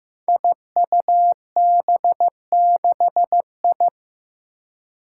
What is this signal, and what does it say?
Morse code, heard as a keyed tone, about 15 wpm, 700 Hz. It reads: IUB6I
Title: Five Challenging Letter-Number Combinations 15wpm